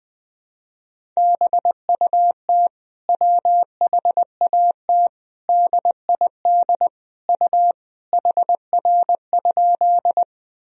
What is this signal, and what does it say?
Morse code, heard as a keyed tone, about 21 wpm, 700 Hz. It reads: BUT WHAT DID U HR?
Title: Sentences from Top 300 Words and Abbreviations 20wpm